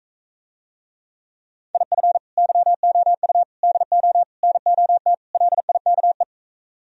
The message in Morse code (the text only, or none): IF YOU DO NOT LIKE